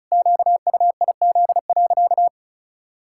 Morse code, heard as a keyed tone, about 36 wpm, 700 Hz. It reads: QUIZ.
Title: Pangram Challenge v2 - 35wpm